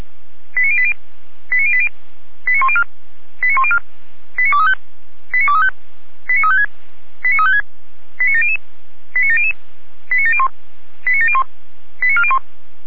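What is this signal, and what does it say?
Five-tone selective call (ZVEI), heard as quick runs of five tones, tones of about 70 ms. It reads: 89080 89080 89104 89104 89235 89235 89356 89356 89700 89700 89701 89701 89401
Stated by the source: Five-tone ZVEI selective call received on an SDR.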